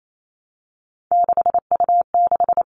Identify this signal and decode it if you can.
Morse code, heard as a keyed tone, about 28 wpm, 700 Hz. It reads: 6U6
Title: All Cut Numbers - 3-Digit 28wpm